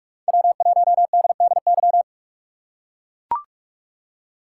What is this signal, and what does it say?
Morse code, heard as a keyed tone, about 44 wpm, 700 Hz. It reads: W1DDY